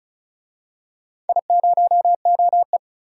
Morse code, heard as a keyed tone, about 36 wpm, 700 Hz. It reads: I0OE
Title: DXCC-100 Call Signs - 4 Characters - 35wpm